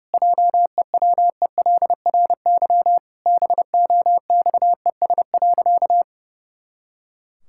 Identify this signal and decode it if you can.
Morse code, heard as a keyed tone, about 30 wpm, 700 Hz. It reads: JEWELRY BOXES.
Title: Pangram Challenge v2 - 30wpm